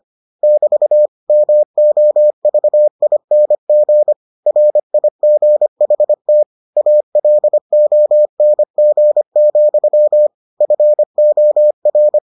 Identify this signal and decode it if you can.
Morse code, heard as a keyed tone, about 25 wpm, 600 Hz. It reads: <BT> MOVING RIGHT ALONG, FOR